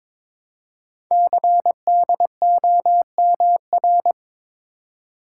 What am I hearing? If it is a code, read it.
Morse code, heard as a keyed tone, about 23 wpm, 700 Hz. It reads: CDOMR
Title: Code Groups - 5 Letters - 22wpm